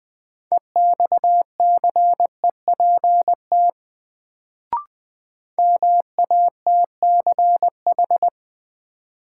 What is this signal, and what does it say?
Morse code, heard as a keyed tone, about 20 wpm, 700 Hz. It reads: EXCEPT MATCH
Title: Sentences from Top 1600 Words - New Word Drill Alt - 20wpm